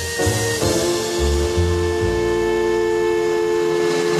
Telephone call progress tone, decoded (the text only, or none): dial tone